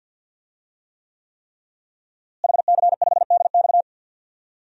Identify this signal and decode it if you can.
Morse code, heard as a keyed tone, about 48 wpm, 700 Hz. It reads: HC5DX
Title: DXCC-100 Call Signs - 5 Characters - Repeated 3x - 50wpm